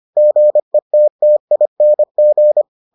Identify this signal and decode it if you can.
Morse code, heard as a keyed tone, about 25 wpm, 600 Hz. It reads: GETTING